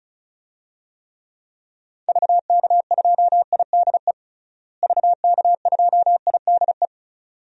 Morse code, heard as a keyed tone, about 36 wpm, 700 Hz. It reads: VK2IDE VK2IDE
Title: DXCC-100 Call Signs - 6 Characters - Repeated 2x - 35wpm